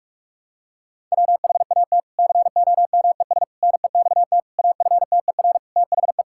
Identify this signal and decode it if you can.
Morse code, heard as a keyed tone, about 46 wpm, 700 Hz. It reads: WHAT COMES NEXT AFTER THE
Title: Sentences from Top 1000 Words - Review Part 5 - 45wpm